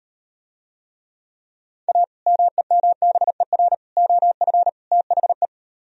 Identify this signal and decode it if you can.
Morse code, heard as a keyed tone, about 38 wpm, 700 Hz. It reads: A MEMBER OF THE